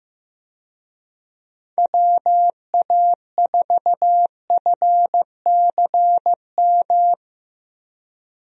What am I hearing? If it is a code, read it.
Morse code, heard as a keyed tone, about 15 wpm, 700 Hz. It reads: WA4FCM
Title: USA Call Signs - 6 Characters - 15wpm